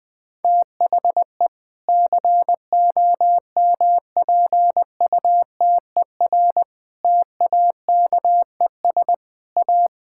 Morse code, heard as a keyed tone, about 20 wpm, 700 Hz. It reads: THE COMPUTER TAKES A
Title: Sentences from Top 1000 Words - Review Part 6 - 20wpm